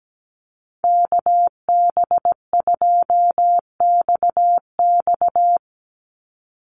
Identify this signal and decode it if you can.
Morse code, heard as a keyed tone, about 17 wpm, 700 Hz. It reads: KB2XX
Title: NA SOTA Chaser Callsigns 17wpm